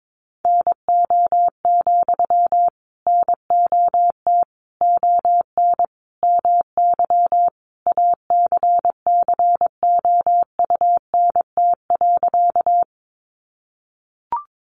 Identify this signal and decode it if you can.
Morse code, heard as a keyed tone, about 22 wpm, 700 Hz. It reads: NO, NOT ON MY ACCOUNT.
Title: Sentences from Top 900 Words 22wpm